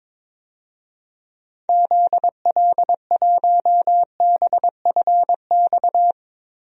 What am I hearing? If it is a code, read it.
Morse code, heard as a keyed tone, about 22 wpm, 700 Hz. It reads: ZL1BFX
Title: DXCC-100 Call Signs - 6 Characters - 22wpm